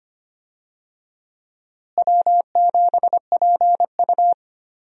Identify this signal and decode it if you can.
Morse code, heard as a keyed tone, about 25 wpm, 700 Hz. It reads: W7PU
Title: USA Call Signs - 4 Characters - Repeated 2x - 25wpm